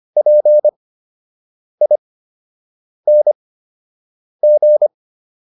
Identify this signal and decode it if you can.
Morse code, heard as a keyed tone, about 25 wpm, 600 Hz. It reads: P I N G